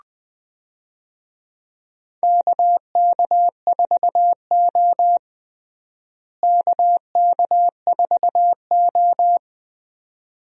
Morse code, heard as a keyed tone, about 21 wpm, 700 Hz. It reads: KK4O KK4O
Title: USA Call Signs - 4 Characters - Repeated 2x - 20wpm